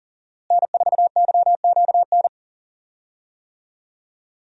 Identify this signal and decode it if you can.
Morse code, heard as a keyed tone, about 40 wpm, 700 Hz. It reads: N4YQN
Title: USA Call Signs - 5 Characters - Repeated 3x - 40wpm